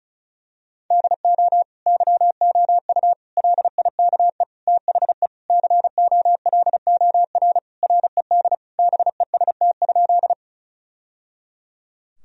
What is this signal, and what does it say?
Morse code, heard as a keyed tone, about 34 wpm, 700 Hz. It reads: DO YOU LIKE THE COLOR RED BEST?
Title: Sentences from Top 1000 Words - Review Part 2 - 35wpm